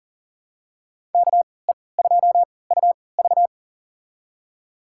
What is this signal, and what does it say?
Morse code, heard as a keyed tone, about 40 wpm, spaced as for 23 wpm, 700 Hz. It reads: KE2UV
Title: USA Call Signs - 5 Characters - 3x Character Spacing - 40wpm